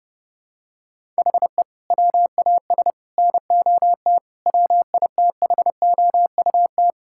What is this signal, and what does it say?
Morse code, heard as a keyed tone, about 30 wpm, 700 Hz. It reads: HE WAS NOT WITHOUT
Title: Sentences from Top 1000 Words - Review Part 6 - 30wpm